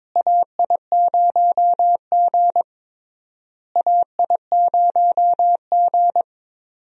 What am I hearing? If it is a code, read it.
Morse code, heard as a keyed tone, about 21 wpm, 700 Hz. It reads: AI0G AI0G
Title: USA Call Signs - 4 Characters - Repeated 3x - 22wpm